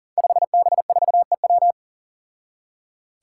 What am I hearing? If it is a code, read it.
Morse code, heard as a keyed tone, about 40 wpm, 700 Hz. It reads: 5B4EW
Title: Five Challenging Letter-Number Combinations 40wpm